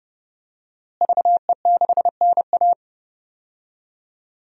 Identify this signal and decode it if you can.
Morse code, heard as a keyed tone, about 30 wpm, 700 Hz. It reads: VE6NA
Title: DXCC-100 Call Signs - 5 Characters - Repeated 2x - 30wpm